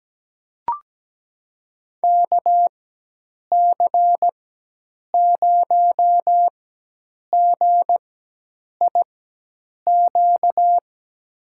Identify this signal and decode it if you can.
Morse code, heard as a keyed tone, about 17 wpm, spaced as for 8 wpm, 700 Hz. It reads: KC0GIQ
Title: USA Call Signs - 6 Characters - 4x Character Spacing - 17wpm